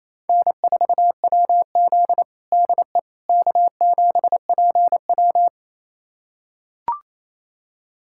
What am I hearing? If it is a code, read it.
Morse code, heard as a keyed tone, about 27 wpm, 700 Hz. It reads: N4WZ DE K7PW
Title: USA Callsign DE Callsign - 4 Characters - 28wpm